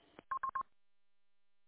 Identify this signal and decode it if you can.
There is no code.